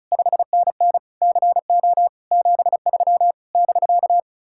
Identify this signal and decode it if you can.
Morse code, heard as a keyed tone, about 45 wpm, 700 Hz. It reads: 5NN CO 73 <BK>